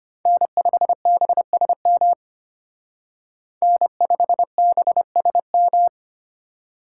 Morse code, heard as a keyed tone, about 30 wpm, 700 Hz. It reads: N5BSM N5BSM
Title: Speed Racing - USA 1x3 Call Signs 25wpm